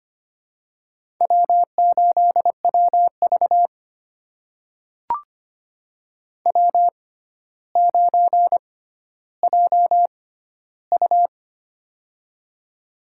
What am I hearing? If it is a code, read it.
Morse code, heard as a keyed tone, about 25 wpm, spaced as for 13 wpm, 700 Hz. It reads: W8WV W9JU
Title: USA Call Signs - 4 Characters - 6x Character Spacing - 25wpm